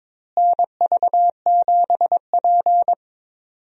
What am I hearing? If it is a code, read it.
Morse code, heard as a keyed tone, about 22 wpm, 700 Hz. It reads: NV7P
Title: USA Call Signs - 4 Characters - Repeated 3x - 22wpm